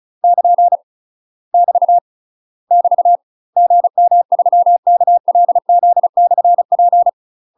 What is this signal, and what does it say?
Morse code, heard as a keyed tone, about 34 wpm, 700 Hz. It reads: <KN> <BT> <BT> GM3KLZ/P
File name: UK MASTER QSOs TRAINING / 35WPM / UK MASTER QSOs 0004